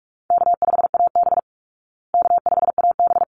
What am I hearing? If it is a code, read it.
Morse code, heard as a keyed tone, about 44 wpm, 700 Hz. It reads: K5AB K5AB
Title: Call Signs Repeated 3x - 45wpm